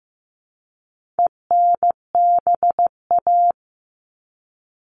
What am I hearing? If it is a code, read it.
Morse code, heard as a keyed tone, about 14 wpm, 700 Hz. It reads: ENBA
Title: All Cut Numbers - 4-Digit 15wpm